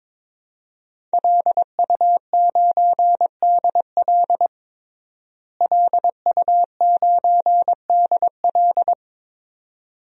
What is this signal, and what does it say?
Morse code, heard as a keyed tone, about 22 wpm, 700 Hz. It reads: LU9DL LU9DL
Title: DXCC-100 Call Signs - 5 Characters - Repeated 2x - 22wpm